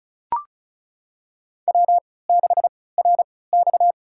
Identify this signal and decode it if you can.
Morse code, heard as a keyed tone, about 36 wpm, spaced as for 20 wpm, 700 Hz. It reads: W6RX
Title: USA Call Signs - 4 Characters - 3x Character Spacing - 35wpm